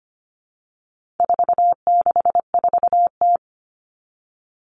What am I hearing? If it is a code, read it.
Morse code, heard as a keyed tone, about 25 wpm, 700 Hz. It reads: 464T
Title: All Cut Numbers - 4-Digit 25wpm